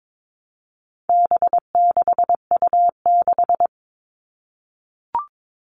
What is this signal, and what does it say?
Morse code, heard as a keyed tone, about 22 wpm, 700 Hz. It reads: B6U6
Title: All Cut Numbers - 4-Digit 22wpm